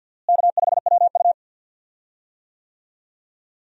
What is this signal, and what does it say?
Morse code, heard as a keyed tone, about 48 wpm, 700 Hz. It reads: K5WU
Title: USA Call Signs - 4 Characters - Repeated 3x - 50wpm